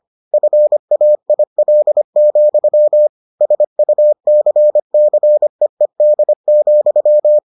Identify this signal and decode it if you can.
Morse code, heard as a keyed tone, about 25 wpm, 600 Hz. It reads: FAIL, SUCCEED,